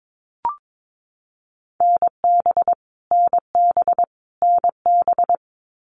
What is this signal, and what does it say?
Morse code, heard as a keyed tone, about 22 wpm, 700 Hz. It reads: NB NB NB
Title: Canadian Provinces and Territories - Repeated 3x - 22wpm